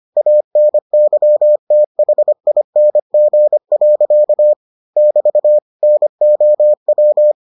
Morse code, heard as a keyed tone, about 25 wpm, 600 Hz. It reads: ANYTHING. <BT> NOW